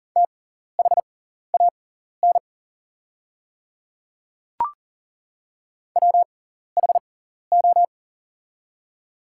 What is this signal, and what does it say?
Morse code, heard as a keyed tone, about 40 wpm, spaced as for 14 wpm, 700 Hz. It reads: THAN WHO
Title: 6x Character Spacing - Top 100 Words - 40wpm